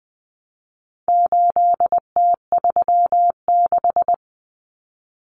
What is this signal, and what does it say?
Morse code, heard as a keyed tone, about 20 wpm, 700 Hz. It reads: 8T36
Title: Common Cut Numbers - 4-Digit 20wpm